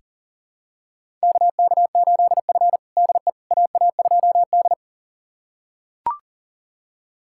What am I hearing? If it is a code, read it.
Morse code, heard as a keyed tone, about 40 wpm, 700 Hz. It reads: KK8F DE AA2D
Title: USA Callsign DE Callsign - 4 Characters - 40wpm